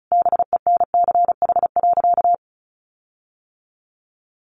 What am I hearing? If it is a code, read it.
Morse code, heard as a keyed tone, about 34 wpm, 700 Hz. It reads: BENCH.
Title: Sentences from Top 1400 Words 35wpm